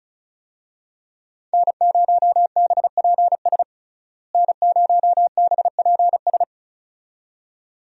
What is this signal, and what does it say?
Morse code, heard as a keyed tone, about 34 wpm, 700 Hz. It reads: N0BPS N0BPS
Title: USA Call Signs - 5 Characters - Repeated 2x - 35wpm